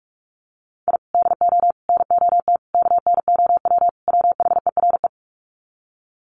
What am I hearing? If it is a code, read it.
Morse code, heard as a keyed tone, about 44 wpm, 700 Hz. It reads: I DO NOT KNOW WHERE
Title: Sets of 5 Words 45wpm